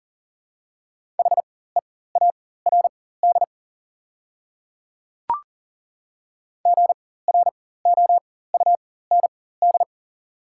Morse code, heard as a keyed tone, about 40 wpm, spaced as for 19 wpm, 690 Hz. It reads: HEARD GROUND
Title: 4x Character Spacing - Top 500 Words - 40wpm